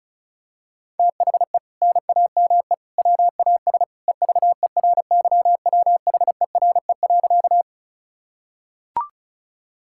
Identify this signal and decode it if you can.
Morse code, heard as a keyed tone, about 36 wpm, 700 Hz. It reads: THE NAME WAS EVERYWHERE.
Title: Sentences from Top 1600 Words 35wpm